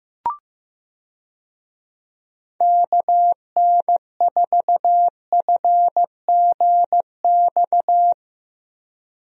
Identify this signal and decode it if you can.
Morse code, heard as a keyed tone, about 15 wpm, 700 Hz. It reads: KN4FGX
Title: USA Call Signs - 6 Characters - Repeated 2x - 15wpm